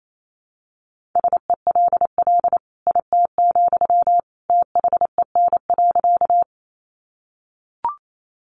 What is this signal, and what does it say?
Morse code, heard as a keyed tone, about 28 wpm, 700 Hz. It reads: SELL IT, THEN.
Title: Sentences from Top 1100 Words 28wpm